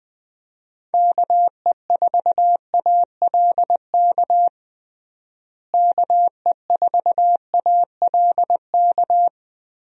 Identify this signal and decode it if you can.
Morse code, heard as a keyed tone, about 20 wpm, 700 Hz. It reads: KE4ALK KE4ALK
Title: USA Call Signs - 6 Characters - Repeated 3x - 20wpm